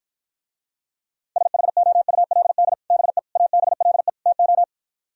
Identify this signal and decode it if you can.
Morse code, heard as a keyed tone, about 52 wpm, 700 Hz. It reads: SHOULD BE ABLE TO